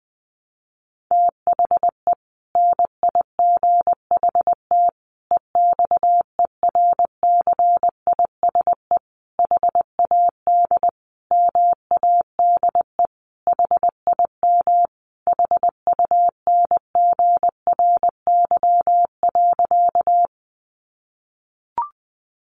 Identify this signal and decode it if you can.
Morse code, heard as a keyed tone, about 20 wpm, 700 Hz. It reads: THE NIGHT EXERCISE HAD MADE HIM HUNGRY.
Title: Sentences from Top 1400 Words 20wpm